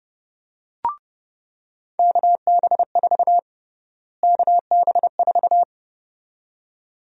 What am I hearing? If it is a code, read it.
Morse code, heard as a keyed tone, about 30 wpm, 700 Hz. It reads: KB4 KB4
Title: USA Call Sign Prefixes - Repeated 2x - 30wpm